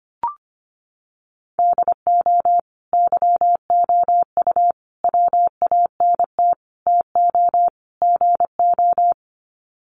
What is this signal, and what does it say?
Morse code, heard as a keyed tone, about 25 wpm, 700 Hz. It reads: DO YOU WANT TO GO
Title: Sets of 5 Words Encore2 25wpm